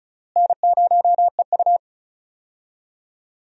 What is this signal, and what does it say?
Morse code, heard as a keyed tone, about 36 wpm, 700 Hz. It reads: N0EU
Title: USA Call Signs - 4 Characters - Repeated 3x - 35wpm